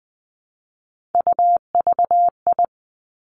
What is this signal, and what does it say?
Morse code, heard as a keyed tone, about 20 wpm, 700 Hz. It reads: UVI
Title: Three Challenging Letter-Number Combinations 20wpm